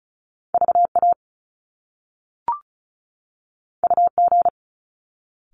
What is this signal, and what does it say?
Morse code, heard as a keyed tone, about 36 wpm, 700 Hz. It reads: VA UG
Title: Top Two Letter Combinations in Words 35wpm